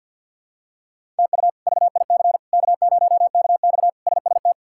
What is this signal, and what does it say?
Morse code, heard as a keyed tone, about 49 wpm, 700 Hz. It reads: TU VIC K0KX SST
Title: K1USN SST QSO Practice - 50wpm